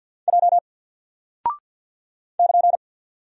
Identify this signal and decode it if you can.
Morse code, heard as a keyed tone, about 48 wpm, 700 Hz. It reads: J <KN>
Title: Single Character (Letter:Number:Punctuation:Prosign) Alt 50wpm